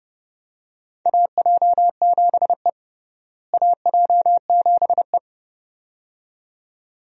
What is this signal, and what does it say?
Morse code, heard as a keyed tone, about 31 wpm, 700 Hz. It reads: AJ7E AJ7E
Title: USA Call Signs - 4 Characters - Repeated 2x - 30wpm